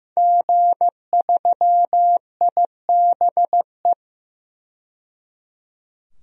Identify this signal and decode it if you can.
Morse code, heard as a keyed tone, about 15 wpm, 700 Hz. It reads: G3IBE
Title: DXCC-100 Call Signs - 5 Characters - 15wpm